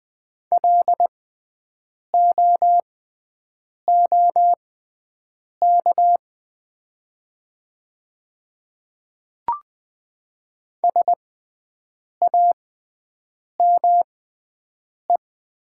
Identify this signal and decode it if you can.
Morse code, heard as a keyed tone, about 20 wpm, spaced as for 7 wpm, 700 Hz. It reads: LOOK SAME
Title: 6x Character Spacing - Top 300 Words - 20wpm